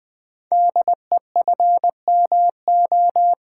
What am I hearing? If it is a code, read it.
Morse code, heard as a keyed tone, about 20 wpm, 700 Hz. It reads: DEFMO
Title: Code Groups - 5 Letters - 20wpm